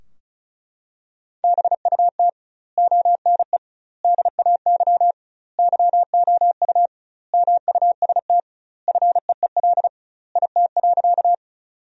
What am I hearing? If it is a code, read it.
Morse code, heard as a keyed tone, about 36 wpm, 700 Hz. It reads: BUT ONE DAY YOU MUST FEEL IT.
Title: 2x Word-Spacing - Sentences from Top 400 Words - 35wpm